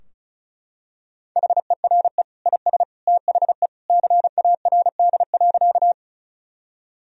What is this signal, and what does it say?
Morse code, heard as a keyed tone, about 36 wpm, 700 Hz. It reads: HERE IS THE CARD.
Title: Sentences from Top 1000 Words - Review Part 4 - 35wpm